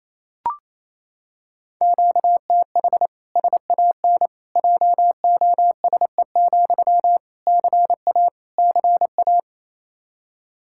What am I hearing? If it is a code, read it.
Morse code, heard as a keyed tone, about 28 wpm, 700 Hz. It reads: QTH SAN JOSE, CA CA
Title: Practice QSO - Exchange 1 - Beginning - 28wpm